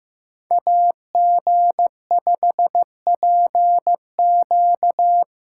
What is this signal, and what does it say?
Morse code, heard as a keyed tone, about 18 wpm, 700 Hz. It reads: AG5PQ
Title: Speed Racing - USA 2x2 Call Signs 15wpm